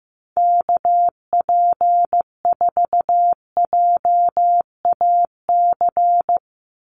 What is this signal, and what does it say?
Morse code, heard as a keyed tone, about 15 wpm, 700 Hz. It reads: KP4JAC
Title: USA Call Signs 15wpm